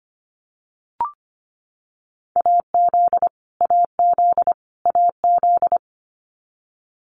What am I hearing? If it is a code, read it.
Morse code, heard as a keyed tone, about 25 wpm, 700 Hz. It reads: AZ AZ AZ
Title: US States Repeated 3x 25wpm